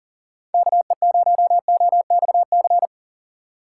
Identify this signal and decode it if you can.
Morse code, heard as a keyed tone, about 39 wpm, 700 Hz. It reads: KE0OXC
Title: USA Call Signs - 6 Characters - 4x Character Spacing - 40wpm